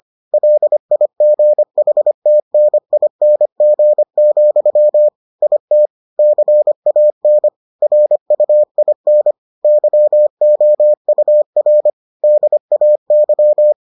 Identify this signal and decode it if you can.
Morse code, heard as a keyed tone, about 25 wpm, 600 Hz. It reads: LIGHTNING, IT CAN RUIN YOUR DAY